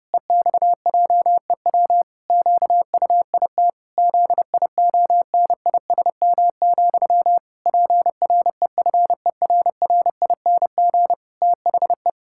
Morse code, heard as a keyed tone, about 32 wpm, 700 Hz. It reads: EXJEW QUIT ZIONISM, PREFERRING THE
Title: Pangram Challenge v2 - 30wpm